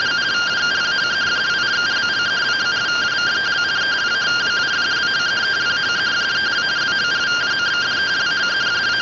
Two-tone teleprinter signal, about 170 Hz shift.
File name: rtty rx